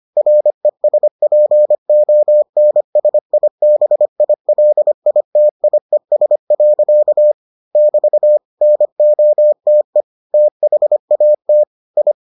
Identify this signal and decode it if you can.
Morse code, heard as a keyed tone, about 25 wpm, 600 Hz. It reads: RESPONSIBILITIES. <BT> NOTE THAT I